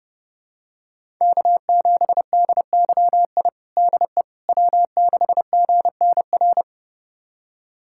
Morse code, heard as a keyed tone, about 29 wpm, 700 Hz. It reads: K7DYI DE W6GNR
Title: USA Callsign DE Callsign - 5 Characters - 30wpm